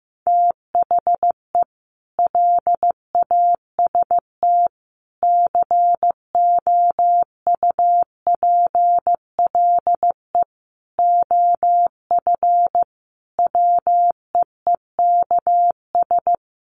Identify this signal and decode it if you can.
Morse code, heard as a keyed tone, about 15 wpm, 700 Hz. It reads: THE LAST COUPLE OF WEEKS
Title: Sets of 5 Words Encore2 15wpm